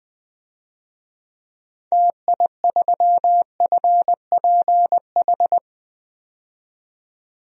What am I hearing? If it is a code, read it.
Morse code, heard as a keyed tone, about 21 wpm, 700 Hz. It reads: TI3FPH
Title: DXCC-100 Call Signs - 6 Characters - 20wpm